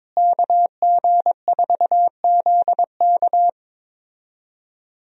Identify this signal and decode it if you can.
Morse code, heard as a keyed tone, about 22 wpm, 700 Hz. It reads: KG4ZK
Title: USA Call Signs - 5 Characters - Repeated 3x - 22wpm